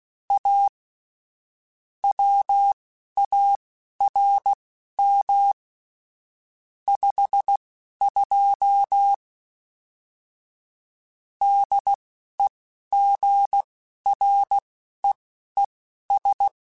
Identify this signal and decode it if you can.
Morse code, heard as a keyed tone, about 16 wpm, spaced as for 11 wpm, 800 Hz. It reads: A WARM 52 DEGREES
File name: Weather Sets / Weather Sets-mp3s / Weather 4 10 @10